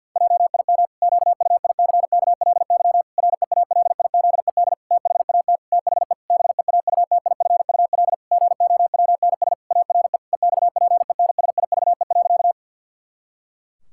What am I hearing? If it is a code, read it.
Morse code, heard as a keyed tone, about 50 wpm, 700 Hz. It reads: JIM QUICKLY REALIZED THAT THE BEAUTIFUL GOWNS ARE EXPENSIVE.
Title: Pangram Challenge v2 - 50wpm